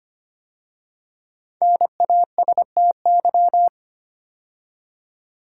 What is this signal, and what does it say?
Morse code, heard as a keyed tone, about 25 wpm, 700 Hz. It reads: NASTY
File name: 5 Letter Words - First 3 Letters Given - 25wpm